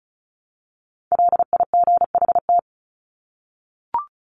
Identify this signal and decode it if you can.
Morse code, heard as a keyed tone, about 34 wpm, 700 Hz. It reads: LIGHT
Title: Top 300 Words Repeated 35wpm